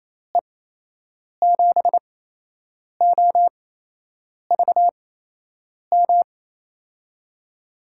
Morse code, heard as a keyed tone, about 28 wpm, spaced as for 8 wpm, 700 Hz. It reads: E7OVM
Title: USA Call Signs - 8x Character Spacing - 28wpm